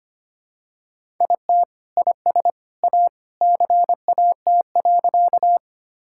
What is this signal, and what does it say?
Morse code, heard as a keyed tone, about 25 wpm, 700 Hz. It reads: IT IS A CAT.
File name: practice / 25wpm / Sentences with 3-Word Phrases - 25wpm